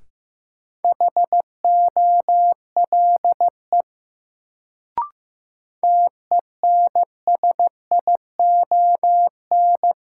Morse code, heard as a keyed tone, about 15 wpm, 700 Hz. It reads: HOLE TENSION
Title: Sentences from Top 1600 Words - New Word Drill Alt - 15wpm